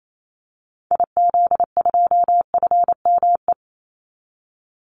Morse code, heard as a keyed tone, about 28 wpm, 700 Hz. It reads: IZ2FME
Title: Call Signs Repeated 2x - 28wpm